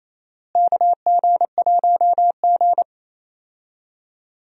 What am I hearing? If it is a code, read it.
Morse code, heard as a keyed tone, about 27 wpm, 700 Hz. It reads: KG1G
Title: USA Call Signs - 4 Characters - Repeated 2x - 28wpm